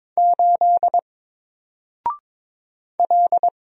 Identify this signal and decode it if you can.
Morse code, heard as a keyed tone, about 22 wpm, 700 Hz. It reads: 8 L
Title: Single Letter-Number 22wpm